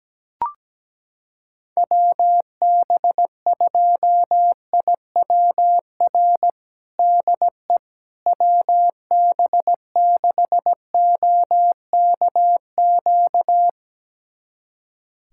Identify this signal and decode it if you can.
Morse code, heard as a keyed tone, about 17 wpm, 700 Hz. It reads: WB2IWR DE WB6OKQ
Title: USA Callsign DE Callsign 17wpm